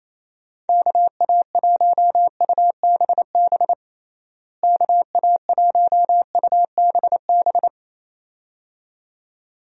Morse code, heard as a keyed tone, about 28 wpm, 700 Hz. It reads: KA1UBB KA1UBB
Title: USA Call Signs - 6 Characters - Repeated 2x - 28wpm